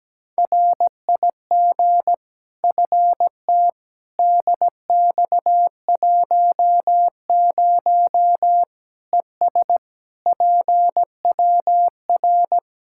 Morse code, heard as a keyed tone, about 17 wpm, 700 Hz. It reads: RIG FT DX10 ES PWR
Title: Practice QSO - Exchange 2 - Info - 17wpm